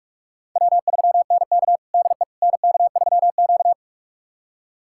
Morse code, heard as a keyed tone, about 41 wpm, 700 Hz. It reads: W3NK DE NK3Q
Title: USA Callsign DE Callsign - 4 Characters - 45wpm